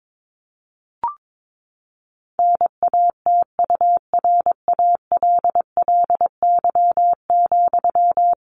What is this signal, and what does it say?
Morse code, heard as a keyed tone, about 22 wpm, 700 Hz. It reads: NATURALLY,
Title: Sentences from Top 1500 Words 22wpm